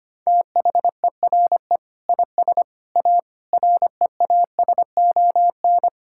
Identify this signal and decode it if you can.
Morse code, heard as a keyed tone, about 25 wpm, 700 Hz. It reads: THERE IS A REASON
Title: Sentences from Top 1000 Words - Review Part 5 - 25wpm